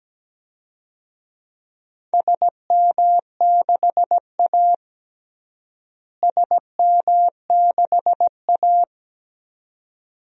Morse code, heard as a keyed tone, about 17 wpm, 700 Hz. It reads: SM6A SM6A
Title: DXCC-100 Call Signs - 4 Characters - Repeated 2x - 17wpm